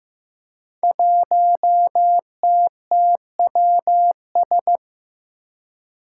Morse code, heard as a keyed tone, about 15 wpm, 700 Hz. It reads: 1TTWS
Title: Five Challenging Letter-Number Combinations 15wpm